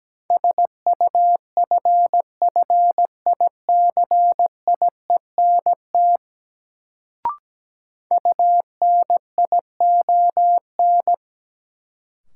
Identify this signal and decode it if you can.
Morse code, heard as a keyed tone, about 17 wpm, 700 Hz. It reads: SUFFICIENT UNION
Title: Sentences from Top 1400 Words - New Word Drill Alt - 17wpm